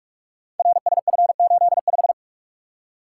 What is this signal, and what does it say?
Morse code, heard as a keyed tone, about 44 wpm, 700 Hz. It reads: RSF85
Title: Code Groups - 5 Letter:Numbers - 45wpm